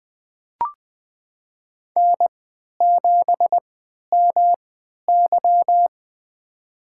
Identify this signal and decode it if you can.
Morse code, heard as a keyed tone, about 20 wpm, spaced as for 11 wpm, 700 Hz. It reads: N7MY
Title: USA Call Signs - 4 Characters - 3x Character Spacing - 20wpm